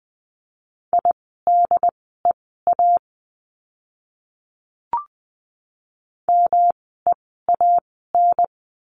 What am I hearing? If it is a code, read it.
Morse code, heard as a keyed tone, about 20 wpm, spaced as for 15 wpm, 700 Hz. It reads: IDEA MEAN
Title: 108 Lesson 18f - Letter m - Words 20wpm